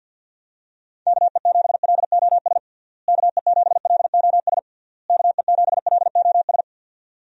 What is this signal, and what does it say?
Morse code, heard as a keyed tone, about 51 wpm, 700 Hz. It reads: KE7LOS KE7LOS KE7LOS
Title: USA Call Signs - 6 Characters - Repeated 3x - 50wpm